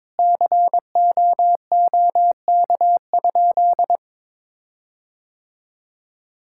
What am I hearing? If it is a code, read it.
Morse code, heard as a keyed tone, about 22 wpm, 700 Hz. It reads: COOK?
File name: Sentences with 7-Word Phrases - 22wpm